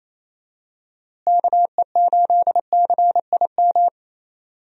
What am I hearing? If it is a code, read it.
Morse code, heard as a keyed tone, about 28 wpm, 700 Hz. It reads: KE8CIM
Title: USA Call Signs - 6 Characters - Repeated 3x - 28wpm